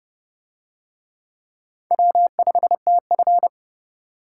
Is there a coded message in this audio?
Yes: Morse code from 1.9 to 3.5 s, heard as a keyed tone, about 28 wpm, 700 Hz. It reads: W5TF